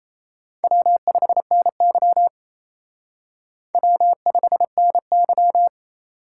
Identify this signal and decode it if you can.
Morse code, heard as a keyed tone, about 34 wpm, 700 Hz. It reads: W5NY W5NY
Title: Speed Racing - USA Call Signs 28wpm